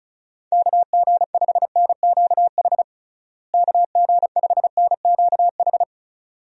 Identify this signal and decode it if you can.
Morse code, heard as a keyed tone, about 36 wpm, 700 Hz. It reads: KG5NQH KG5NQH
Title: USA Call Signs - 6 Characters - Repeated 2x - 35wpm